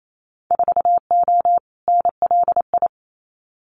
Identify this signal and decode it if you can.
Morse code, heard as a keyed tone, about 28 wpm, 700 Hz. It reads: 4O NLI
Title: Winter Field Day Exchanges Encore 28wpm